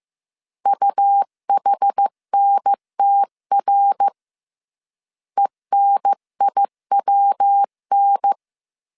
Morse code, heard as a keyed tone, about 15 wpm, 800 Hz. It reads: UHNTR ENIWN
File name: CW / DET / 1 3 1500C S C 14wpm 2458